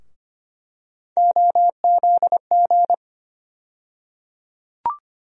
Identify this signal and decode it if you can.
Morse code, heard as a keyed tone, about 25 wpm, 700 Hz. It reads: OZG
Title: USA Call Sign Postfixes - 3 Letters 25wpm